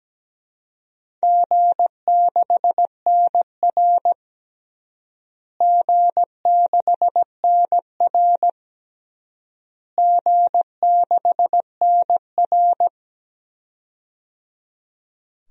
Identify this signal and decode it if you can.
Morse code, heard as a keyed tone, about 17 wpm, 700 Hz. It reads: G6NR G6NR G6NR
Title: DXCC-100 Call Signs - 4 Characters - Repeated 3x - 17wpm